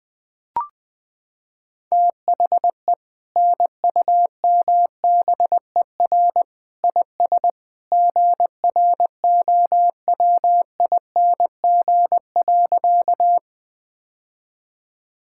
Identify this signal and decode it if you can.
Morse code, heard as a keyed tone, about 20 wpm, 700 Hz. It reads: THE NUMBER IS GROWING.
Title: Sentences with 2-Word Phrases - 20wpm